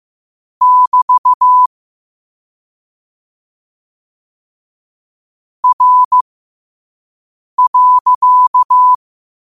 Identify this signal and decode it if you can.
Morse code, heard as a keyed tone, about 15 wpm, spaced as for 5 wpm, 1000 Hz. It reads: <BT> R.
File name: qotdfiles / 5wpm